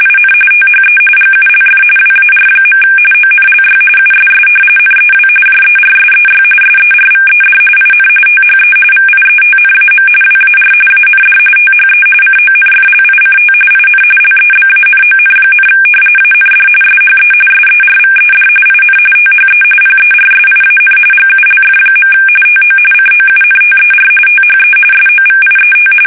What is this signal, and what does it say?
RTTY, about 45.45 baud. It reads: IN THE GULF WAR. ONLY USS NEW JERSEY SERVED IN THE VIETNAM WAR AND THE CONFLICT IN LEBANON, AND SHE SERVED IN THE PERSIAN GULF. TO COUNTER THE SOVIET
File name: fsk850 / battleshipnj